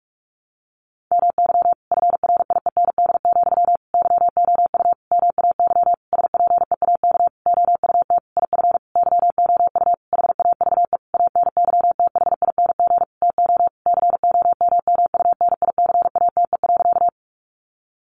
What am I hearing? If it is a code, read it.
Morse code, heard as a keyed tone, about 44 wpm, 700 Hz. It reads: MY FRIEND, YOU MAY SPEAK OUT IF YOU HAVE ANYTHING TO COMMUNICATE.
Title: Sentences from Top 1300 Words 45wpm